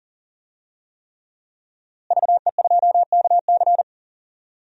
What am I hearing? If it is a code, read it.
Morse code, heard as a keyed tone, about 38 wpm, 700 Hz. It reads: VE2KC
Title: DXCC-100 Call Signs - 5 Characters - Repeated 3x - 40wpm